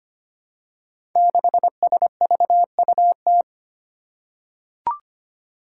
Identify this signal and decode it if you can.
Morse code, heard as a keyed tone, about 25 wpm, 700 Hz. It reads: 6SVUT
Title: Five Challenging Letter-Number Combinations 25wpm